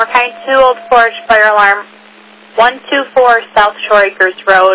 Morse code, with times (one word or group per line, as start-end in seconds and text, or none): none